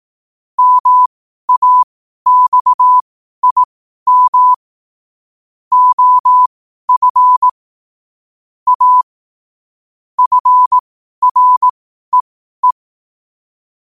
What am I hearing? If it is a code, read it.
Morse code, heard as a keyed tone, about 18 wpm, spaced as for 12 wpm, 1000 Hz. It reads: MAXIM OF A FREE
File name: qotdfiles / 12wpm